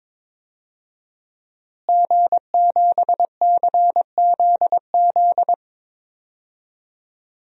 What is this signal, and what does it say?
Morse code, heard as a keyed tone, about 22 wpm, 700 Hz. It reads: G7CZZ
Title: DXCC-100 Call Signs - 5 Characters - Repeated 2x - 22wpm